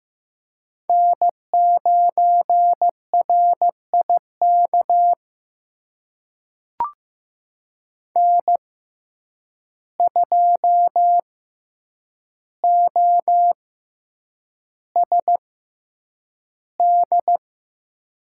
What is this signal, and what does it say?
Morse code, heard as a keyed tone, about 15 wpm, spaced as for 8 wpm, 700 Hz. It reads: N9RIK N2OSD
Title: USA Call Signs - 5 Characters - 6x Character Spacing - 15wpm